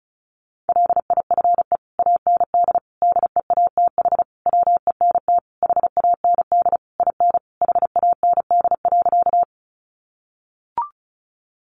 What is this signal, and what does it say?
Morse code, heard as a keyed tone, about 34 wpm, 700 Hz. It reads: LIFE AND DEATH WENT HAND IN HAND.
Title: Sentences from Top 1100 Words 35wpm